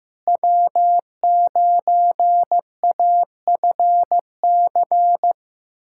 Morse code, heard as a keyed tone, about 15 wpm, 700 Hz. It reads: W9AFC
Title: USA Callsign DE Callsign 15wpm